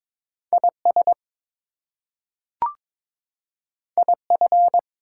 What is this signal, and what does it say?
Morse code, heard as a keyed tone, about 22 wpm, 700 Hz. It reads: IS IF
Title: Two Letter Words - 22wpm